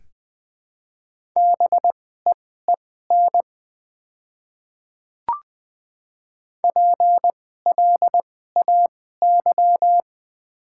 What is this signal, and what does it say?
Morse code, heard as a keyed tone, about 20 wpm, spaced as for 14 wpm, 700 Hz. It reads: BEEN PLAY
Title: 2x Character Spacing - Top 200 Words - 20wpm